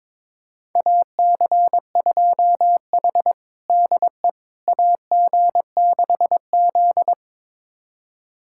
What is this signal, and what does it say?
Morse code, heard as a keyed tone, about 22 wpm, 700 Hz. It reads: AC2H DE AG6Z
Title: USA Callsign DE Callsign - 4 Characters - 22wpm